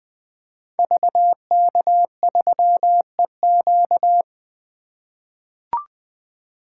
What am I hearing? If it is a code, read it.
Morse code, heard as a keyed tone, about 20 wpm, 700 Hz. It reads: VK3EQ
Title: 2023 SOTA Activator Call Signs - 20wpm